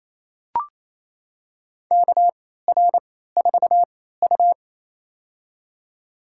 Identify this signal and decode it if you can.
Morse code, heard as a keyed tone, about 28 wpm, spaced as for 16 wpm, 700 Hz. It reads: KR4U
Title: USA Call Signs - 4 Characters - 3x Character Spacing - 28wpm